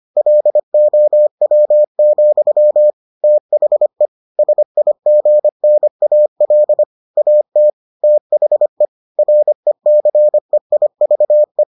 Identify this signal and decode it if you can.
Morse code, heard as a keyed tone, about 25 wpm, 600 Hz. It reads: LOW, THE SIGNAL AT THE RECEIVE